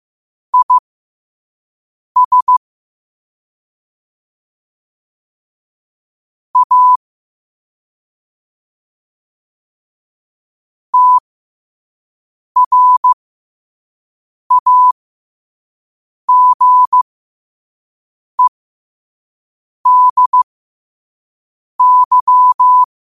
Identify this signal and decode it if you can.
Morse code, heard as a keyed tone, about 15 wpm, spaced as for 5 wpm, 1000 Hz. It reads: IS A TRAGEDY
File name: 5wpm